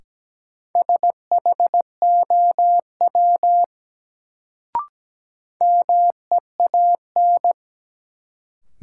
Morse code, heard as a keyed tone, about 17 wpm, 700 Hz. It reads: SHOW MEAN
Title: Sentences from Top 200 Words - New Word Drill - 17wpm